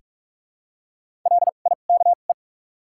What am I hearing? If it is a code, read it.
Morse code, heard as a keyed tone, about 45 wpm, spaced as for 32 wpm, 700 Hz. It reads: LIKE
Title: 2x Character Spacing - Top 200 Words - 45wpm